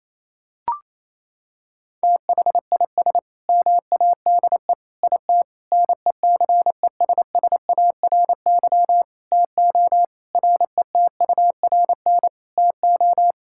Morse code, heard as a keyed tone, about 27 wpm, 700 Hz. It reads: THIS MADE IT NECESSARY TO RETURN TO
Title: Sentences from Top 1600 Words 28wpm